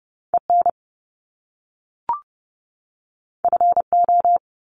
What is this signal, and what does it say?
Morse code, heard as a keyed tone, about 30 wpm, 700 Hz. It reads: EN FO
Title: Top Two Letter Combinations in Words 30wpm